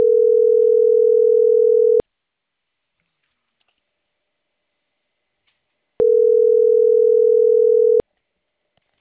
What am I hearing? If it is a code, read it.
Ringback tone (North America), 440 + 480 Hz.